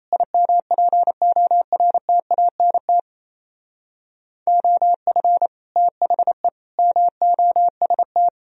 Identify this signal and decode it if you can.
Morse code, heard as a keyed tone, about 33 wpm, 700 Hz. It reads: IMPORTANT OF THE MOST
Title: Speed-Racing - Sets of 4 Words 28wpm